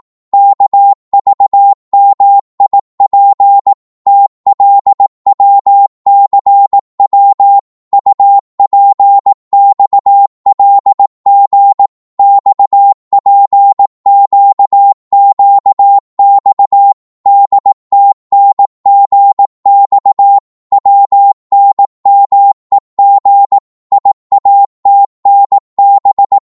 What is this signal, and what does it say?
Morse code, heard as a keyed tone, about 18 wpm, 800 Hz. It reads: KVMIP TLWCW UPXLG XPQQX DTNGX WNMEG IATNB
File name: Morse / ltr-18wpm